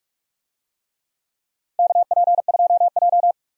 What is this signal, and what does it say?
Morse code, heard as a keyed tone, about 42 wpm, 700 Hz. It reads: KP2J
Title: DXCC-100 Call Signs - 4 Characters - 45wpm